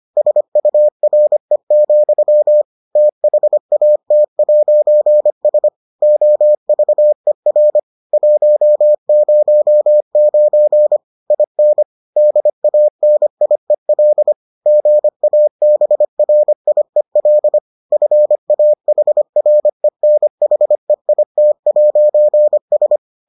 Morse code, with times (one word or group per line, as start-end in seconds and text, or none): 0.2-2.6 SURE,
3.0-5.7 THAT'S
6.0-7.8 OVER
8.1-11.0 109
11.3-11.8 IN
12.2-14.3 DANIEL
14.7-17.6 GABRIEL
17.9-23.0 FAHRENHEIT'S